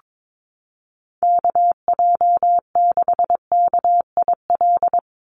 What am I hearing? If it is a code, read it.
Morse code, heard as a keyed tone, about 22 wpm, 700 Hz. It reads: KJ6KIL
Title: USA Call Signs 22wpm